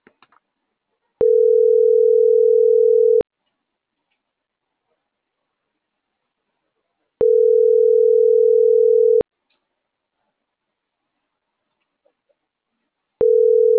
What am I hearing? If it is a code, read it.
Ringback tone (North America), 440 + 480 Hz.